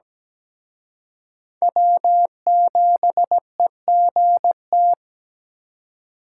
Morse code, heard as a keyed tone, about 20 wpm, 700 Hz. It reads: W7EGT
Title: Speed Racing - USA 1x3 Call Signs 17wpm